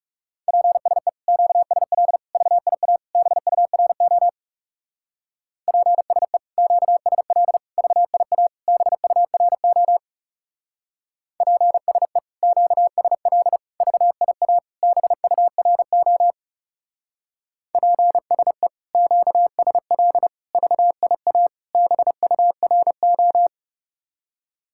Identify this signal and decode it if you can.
Morse code, heard as a keyed tone, about 36 wpm, 700 Hz. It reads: PSE QSL VIA BURO PSE QSL VIA BURO PSE QSL VIA BURO PSE QSL VIA BURO
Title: Speed-Racing Common QSO Phrases - 30wpm